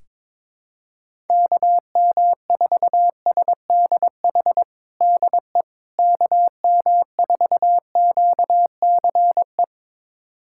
Morse code, heard as a keyed tone, about 22 wpm, 700 Hz. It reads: KM4SDH DE KM4QCE
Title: USA Callsign DE Callsign 22wpm